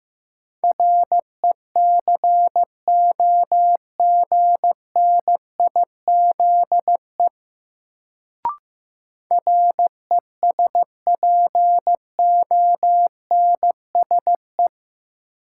Morse code, heard as a keyed tone, about 15 wpm, 700 Hz. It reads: RECOGNIZE RESPONSE
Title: Sentences from Top 1400 Words - New Word Drill Alt - 15wpm